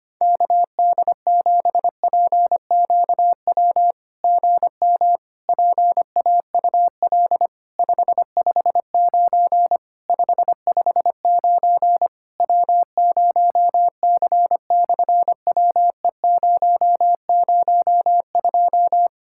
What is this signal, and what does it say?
Morse code, heard as a keyed tone, about 25 wpm, 700 Hz. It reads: KD7PQW GM PAUL 559 559 W0C/WE002
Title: SOTA QSO Practice 25wpm